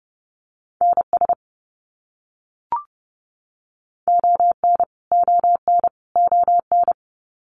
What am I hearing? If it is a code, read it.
Morse code, heard as a keyed tone, about 30 wpm, 700 Hz. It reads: NS ON ON ON
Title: Canadian Provinces and Territories - Repeated 3x - 30wpm